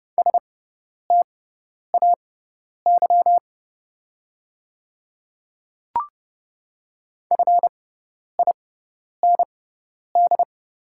Morse code, heard as a keyed tone, about 30 wpm, spaced as for 10 wpm, 690 Hz. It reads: STAY FIND
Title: 6x Character Spacing - Top 500 Words - 30wpm